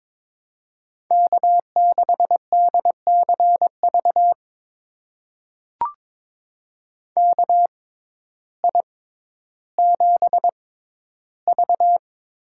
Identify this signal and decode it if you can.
Morse code, heard as a keyed tone, about 22 wpm, spaced as for 12 wpm, 700 Hz. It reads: K6DCV KI7V
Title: USA Call Signs - 6x Character Spacing - 22wpm